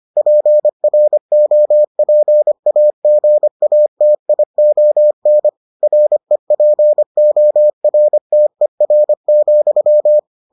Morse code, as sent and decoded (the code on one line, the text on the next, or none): .--. .-. --- .--. .- --. .- - .. --- -. / .-. .--. --- .-. - . .-. --..--
PROPAGATION REPORTER,